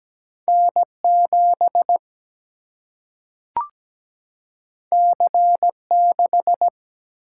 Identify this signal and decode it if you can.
Morse code, heard as a keyed tone, about 17 wpm, 700 Hz. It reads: N7 C6
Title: Single-Letter Single-Number 17wpm